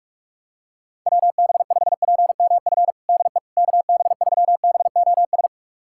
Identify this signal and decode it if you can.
Morse code, heard as a keyed tone, about 41 wpm, 690 Hz. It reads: WB5PMF DE KB3BOS
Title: USA Callsign DE Callsign 45wpm